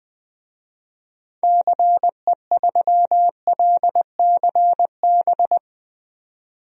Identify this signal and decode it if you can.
Morse code, heard as a keyed tone, about 20 wpm, 700 Hz. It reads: CE3LCB
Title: DXCC-100 Call Signs - 6 Characters - 20wpm